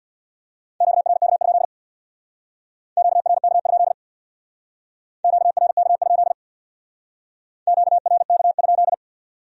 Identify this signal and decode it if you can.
Morse code, heard as a keyed tone, about 62 wpm, 700 Hz. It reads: QRK? QRK? QRK? QRK?
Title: Speed-Racing Q-Codes - 50wpm